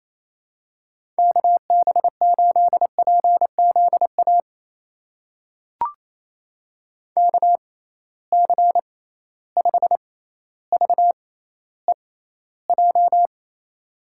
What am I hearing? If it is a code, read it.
Morse code, heard as a keyed tone, about 28 wpm, spaced as for 14 wpm, 700 Hz. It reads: KB8PZA KC5VEJ
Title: USA Call Signs - 6 Characters - 6x Character Spacing - 28wpm